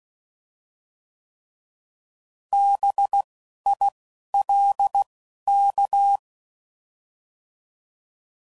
Morse code, heard as a keyed tone, about 16 wpm, 800 Hz. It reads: B I L K